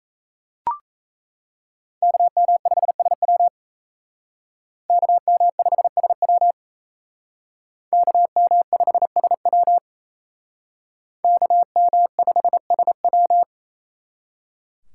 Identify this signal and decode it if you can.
Morse code, heard as a keyed tone, about 35 wpm, 700 Hz. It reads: KM5SW KM5SW KM5SW KM5SW